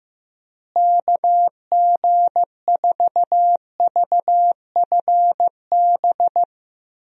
Morse code, heard as a keyed tone, about 15 wpm, 700 Hz. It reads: KG4VFB
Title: USA Call Signs - 6 Characters - Repeated 2x - 15wpm